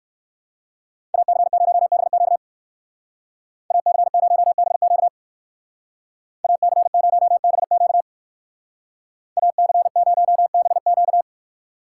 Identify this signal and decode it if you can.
Morse code, heard as a keyed tone, about 56 wpm, 700 Hz. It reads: AC0BQ AC0BQ AC0BQ AC0BQ
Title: Speed Racing - USA Call Signs 45wpm